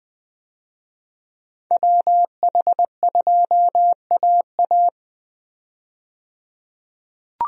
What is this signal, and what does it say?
Morse code, heard as a keyed tone, about 20 wpm, 700 Hz. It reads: WH2AA
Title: USA Call Signs - 5 Characters - 20wpm